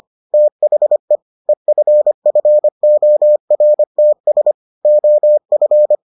Morse code, heard as a keyed tone, about 25 wpm, 600 Hz. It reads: THE EFFORTS OF